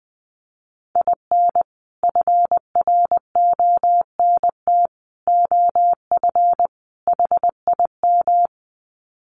Morse code, heard as a keyed tone, about 20 wpm, 700 Hz. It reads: IN FRONT OF HIM
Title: Sets of 4 Words 20wpm